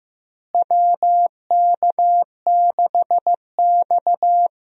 Morse code, heard as a keyed tone, about 15 wpm, 700 Hz. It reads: WK6X
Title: USA Call Signs - 4 Characters - Repeated 2x - 15wpm